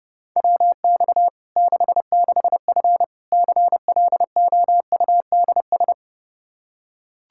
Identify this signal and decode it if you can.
Morse code, heard as a keyed tone, about 32 wpm, 700 Hz. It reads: WX 66F CLOUDS
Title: Practice QSO - Exchange 2 - Info - 30wpm